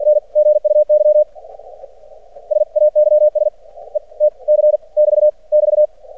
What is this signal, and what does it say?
Morse code, heard as a keyed tone, about 50 wpm, spaced as for 39 wpm, 600 Hz. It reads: N GUY SAYS ETC <BT> <BT>
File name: archived / CRSnet